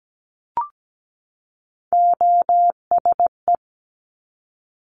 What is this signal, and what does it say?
Morse code, heard as a keyed tone, about 17 wpm, 700 Hz. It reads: OSE
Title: Top Three Letter Combinations in Words 17wpm